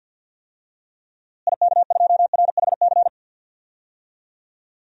Morse code, heard as a keyed tone, about 52 wpm, 700 Hz. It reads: IK2RHC
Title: DXCC-100 Call Signs - 6 Characters - Repeated 2x - 50wpm